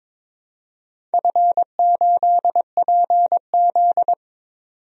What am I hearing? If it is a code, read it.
Morse code, heard as a keyed tone, about 22 wpm, 690 Hz. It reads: F8PZ
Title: DXCC-100 Call Signs - 4 Characters - 22wpm